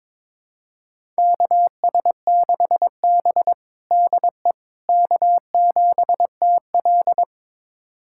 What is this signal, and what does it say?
Morse code, heard as a keyed tone, about 22 wpm, 700 Hz. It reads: KS6B DE K7TL
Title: USA Callsign DE Callsign - 4 Characters - 22wpm